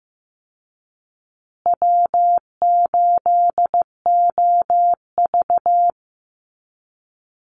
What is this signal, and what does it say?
Morse code, heard as a keyed tone, about 15 wpm, 700 Hz. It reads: W8OV
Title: Easy Call Signs Repeated 3x - 15wpm